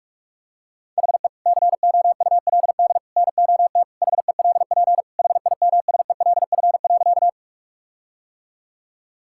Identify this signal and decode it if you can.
Morse code, heard as a keyed tone, about 46 wpm, 700 Hz. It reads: HE COULD NOT HELP HIMSELF.
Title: Sentences from Top 1600 Words 45wpm